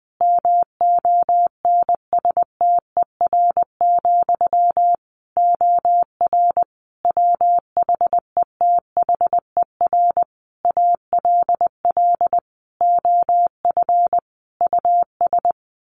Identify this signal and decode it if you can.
Morse code, heard as a keyed tone, about 20 wpm, 700 Hz. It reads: MONSTER, OR WHETHER ALL OF US
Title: A Sound Of Thunder 20wpm